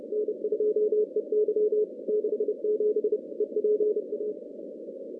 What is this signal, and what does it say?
Morse code, heard as a keyed tone, about 29 wpm, 400 Hz. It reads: N2EY 67 EPA
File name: contests / CW-SS-k5tr- -2100